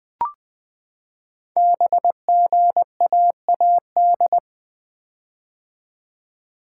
Morse code, heard as a keyed tone, about 20 wpm, 700 Hz. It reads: BGAAD
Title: Code Groups - 5 Letters - 20wpm